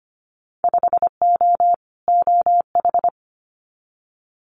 Morse code, heard as a keyed tone, about 25 wpm, 700 Hz. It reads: 5O OH
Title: Winter Field Day Exchanges Encore 25wpm